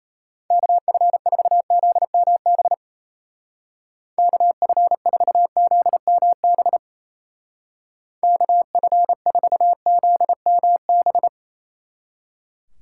Morse code, heard as a keyed tone, about 36 wpm, 700 Hz. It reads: KF4ZMB KF4ZMB KF4ZMB